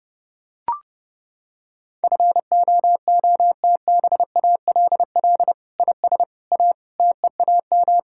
Morse code, heard as a keyed tone, about 30 wpm, 700 Hz. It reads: FOOTBALL IS A TEAM
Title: Sentences from Top 1600 Words 30wpm